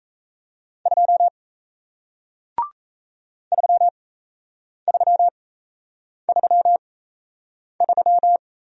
Morse code, heard as a keyed tone, about 38 wpm, spaced as for 8 wpm, 700 Hz. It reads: 2 3333